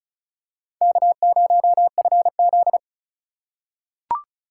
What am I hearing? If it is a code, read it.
Morse code, heard as a keyed tone, about 34 wpm, 700 Hz. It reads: K0FZ
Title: USA Call Signs - 4x Character Spacing - 35wpm